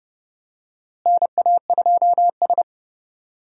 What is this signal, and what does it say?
Morse code, heard as a keyed tone, about 31 wpm, 700 Hz. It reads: NA2S
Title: USA Call Signs - 4 Characters - Repeated 3x - 30wpm